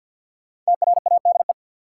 Morse code, heard as a keyed tone, about 51 wpm, 700 Hz. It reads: TRADE
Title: Sets of 5 Words Encore - New Word Drill - 50wpm